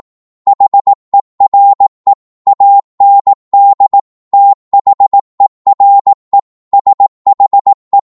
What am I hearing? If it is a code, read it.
Morse code, heard as a keyed tone, about 18 wpm, 800 Hz. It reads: HERE AND THERE SHE
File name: Morse / psg-18wpm